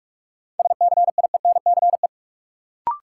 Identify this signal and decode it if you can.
Morse code, heard as a keyed tone, about 42 wpm, 700 Hz. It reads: SCIENCE
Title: Sentences from Top 1000 Words - Review Part 2 - New Word Drill - 45wpm